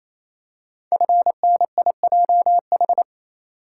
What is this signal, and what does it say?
Morse code, heard as a keyed tone, about 27 wpm, 700 Hz. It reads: FNIJH
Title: Code Groups - 5 Letters - 28wpm